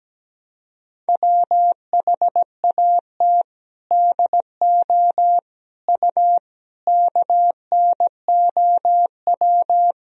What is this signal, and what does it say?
Morse code, heard as a keyed tone, about 17 wpm, 700 Hz. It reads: WHAT DO U KNOW
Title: Sentences from Top 100 Words and Abbreviations 17wpm